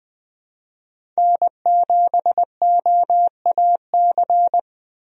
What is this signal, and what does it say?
Morse code, heard as a keyed tone, about 20 wpm, 700 Hz. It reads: N7OAC
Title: USA Call Signs - 5 Characters - 20wpm